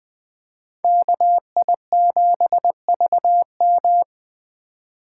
Morse code, heard as a keyed tone, about 20 wpm, 700 Hz. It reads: KI7VM